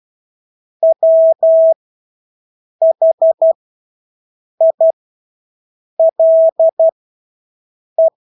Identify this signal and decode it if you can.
Morse code, heard as a keyed tone, about 12 wpm, spaced as for 6 wpm, 650 Hz. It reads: WHILE
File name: Engelske ord 12-6 wpm